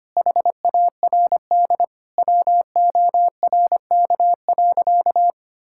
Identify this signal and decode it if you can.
Morse code, heard as a keyed tone, about 25 wpm, 700 Hz. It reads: HARD WORK.
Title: Sentences from Top 1000 Words - Review Part 5 - 25wpm